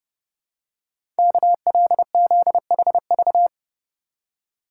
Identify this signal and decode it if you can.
Morse code, heard as a keyed tone, about 30 wpm, 700 Hz. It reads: KLZHV
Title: Code Groups - 5 Letters - 30wpm